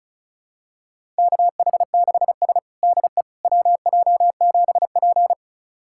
Morse code, heard as a keyed tone, about 36 wpm, 700 Hz. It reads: KH6S DE WJ7P
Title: USA Callsign DE Callsign - 4 Characters - 35wpm